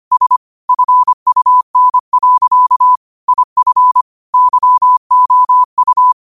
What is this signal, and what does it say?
Morse code, heard as a keyed tone, about 25 wpm, 1000 Hz. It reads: S FUN. IF YOU